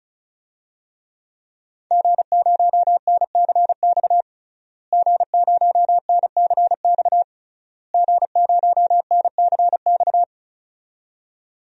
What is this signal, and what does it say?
Morse code, heard as a keyed tone, about 34 wpm, 700 Hz. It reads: G0NCX G0NCX G0NCX
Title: DXCC-100 Call Signs - 5 Characters - Repeated 3x - 35wpm